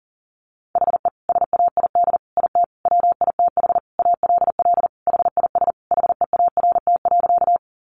Morse code, heard as a keyed tone, about 40 wpm, 700 Hz. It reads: HE SAID IT WITH ALL HIS HEART.
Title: Sentences from Top 1200 Words 40wpm